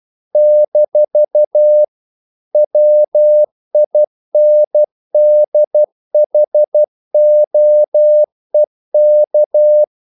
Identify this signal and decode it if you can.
Morse code, heard as a keyed tone, about 12 wpm, 600 Hz. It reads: - WINDHOEK
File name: CW / Practice / NationalCapitals12